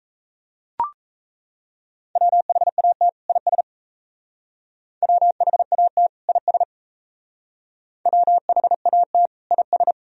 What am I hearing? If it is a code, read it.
Morse code, heard as a keyed tone, about 35 wpm, 690 Hz. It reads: WHAT IS WHAT IS WHAT IS
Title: Speed-Racing - Sets of 2 Words 28wpm